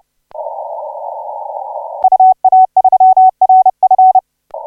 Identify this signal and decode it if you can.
Morse code, heard as a keyed tone, about 30 wpm, 750 Hz. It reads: UA3RF
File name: 5J0R 1807UTC 17m